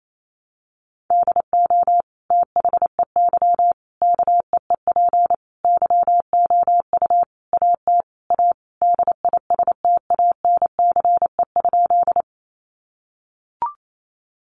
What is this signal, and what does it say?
Morse code, heard as a keyed tone, about 28 wpm, 700 Hz. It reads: DO THEY KEEP YOU AT A DISTANCE?
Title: Sentences from Top 1200 Words 28wpm